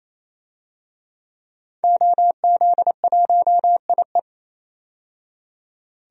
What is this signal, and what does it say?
Morse code, heard as a keyed tone, about 28 wpm, 700 Hz. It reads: OZ1IE
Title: DXCC-100 Call Signs - 5 Characters - 28wpm